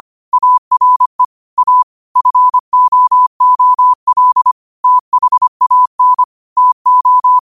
Morse code, heard as a keyed tone, about 25 wpm, 1000 Hz. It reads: ARE A FOOL THAN TO
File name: qotdfiles / 25wpm